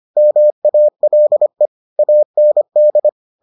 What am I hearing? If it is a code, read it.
Morse code, heard as a keyed tone, about 25 wpm, 600 Hz. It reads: MALE AND